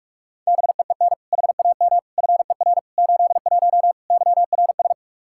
Morse code, heard as a keyed tone, about 46 wpm, 700 Hz. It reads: BEEN HAM FER 81 YRS
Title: Practice QSO - Exchange 3 - More Info - 45wpm